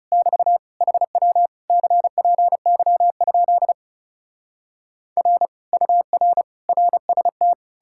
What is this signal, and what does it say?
Morse code, heard as a keyed tone, about 37 wpm, 700 Hz. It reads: <BT> HW CPY? R UR RST
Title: Speed-Racing Common QSO Phrases - 30wpm